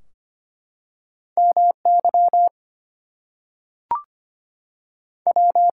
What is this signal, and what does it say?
Morse code, heard as a keyed tone, about 25 wpm, 700 Hz. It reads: MY W
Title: Two Letter Words - 25wpm